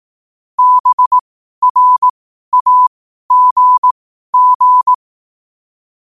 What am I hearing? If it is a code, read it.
Morse code, heard as a keyed tone, about 18 wpm, spaced as for 12 wpm, 1000 Hz. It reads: BRAGG
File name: qotdfiles / 12wpm